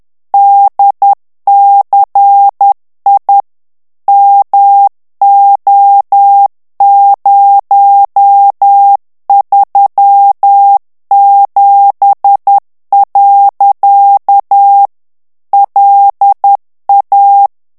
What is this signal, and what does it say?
Morse code, heard as a keyed tone, about 10 wpm, 800 Hz. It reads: DCI MO037. LA
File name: morse mp3 / ITALIEN texte clair 1 vitesse 10 mots minute